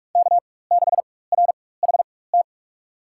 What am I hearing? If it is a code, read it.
Morse code, heard as a keyed tone, about 46 wpm, spaced as for 21 wpm, 700 Hz. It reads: K6RHT
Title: USA Call Signs - 5 Characters - 4x Character Spacing - 45wpm